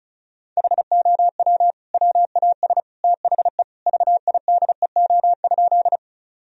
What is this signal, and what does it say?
Morse code, heard as a keyed tone, about 36 wpm, 700 Hz. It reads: HOW WAS THE VIDEO?
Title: Sentences from Top 1000 Words - Review Part 4 - 35wpm